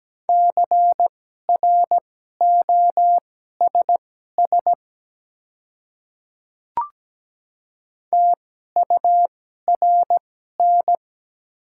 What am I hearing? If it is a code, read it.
Morse code, heard as a keyed tone, about 17 wpm, spaced as for 12 wpm, 700 Hz. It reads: CROSS TURN
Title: 2x Character Spacing - Top 300 Words - 17wpm